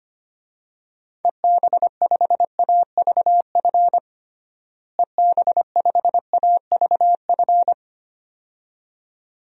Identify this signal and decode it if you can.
Morse code, heard as a keyed tone, about 25 wpm, 700 Hz. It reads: EB5AVF EB5AVF
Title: DXCC-100 Call Signs - 6 Characters - Repeated 2x - 25wpm